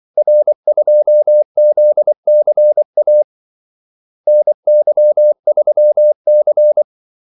Morse code, heard as a keyed tone, about 24 wpm, 600 Hz. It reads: R2ZCA NY3C
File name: Callsigns-24wpm